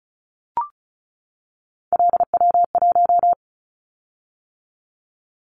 Morse code, heard as a keyed tone, about 34 wpm, 700 Hz. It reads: LW1
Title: Callsign Prefixes for Top 50 DXCC - 35wpm